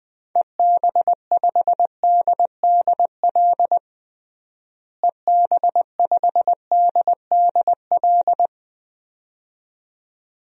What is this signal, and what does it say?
Morse code, heard as a keyed tone, about 20 wpm, 700 Hz. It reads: EB5DDL EB5DDL
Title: DXCC-100 Call Signs - 6 Characters - Repeated 2x - 20wpm